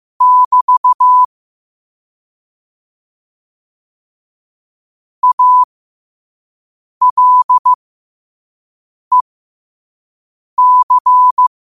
Morse code, heard as a keyed tone, about 15 wpm, spaced as for 5 wpm, 1000 Hz. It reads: <BT> ALEC
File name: qotdfiles / 5wpm